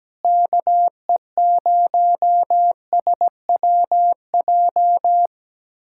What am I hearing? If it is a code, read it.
Morse code, heard as a keyed tone, about 17 wpm, 700 Hz. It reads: KE0SWJ
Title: USA Callsign DE Callsign - 6 Characters - 17wpm